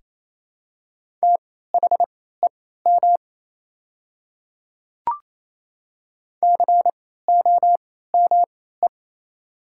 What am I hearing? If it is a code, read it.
Morse code, heard as a keyed tone, about 28 wpm, spaced as for 16 wpm, 700 Hz. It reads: THEM COME
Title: 3x Character Spacing - Top 200 Words - 28wpm